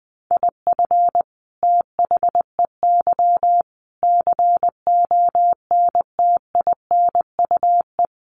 Morse code, heard as a keyed tone, about 20 wpm, 700 Hz. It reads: IF THEY CONTINUE
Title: Sentences from Top 1400 Words 20wpm